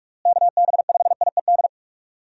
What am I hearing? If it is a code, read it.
Morse code, heard as a keyed tone, about 41 wpm, 700 Hz. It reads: KB5IED
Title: USA Call Signs - 6 Characters - Repeated 2x - 45wpm